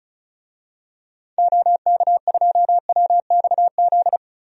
Morse code, heard as a keyed tone, about 34 wpm, 700 Hz. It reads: OK2WXZ
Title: DXCC-100 Call Signs - 6 Characters - 35wpm